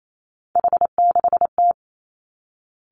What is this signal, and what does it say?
Morse code, heard as a keyed tone, about 28 wpm, 690 Hz. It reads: H6T
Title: Three Challenging Letter-Number Combinations 28wpm